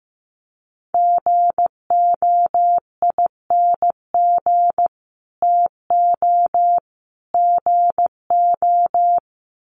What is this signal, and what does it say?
Morse code, heard as a keyed tone, about 15 wpm, 700 Hz. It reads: GOING TO GO
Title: Sets of 3 Words 15wpm